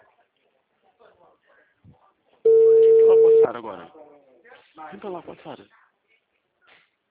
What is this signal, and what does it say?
Ringback tone (China, 450 Hz), with speech.